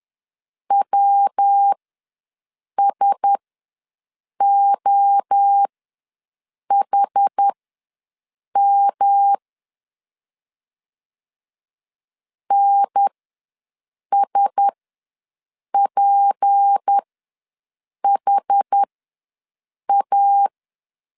Morse code, heard as a keyed tone, about 10 wpm, spaced as for 6 wpm, 800 Hz. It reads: WSOHM NSPHA
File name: LEZIONI / 6 PKX0 / 6 P ws 10wpm 3751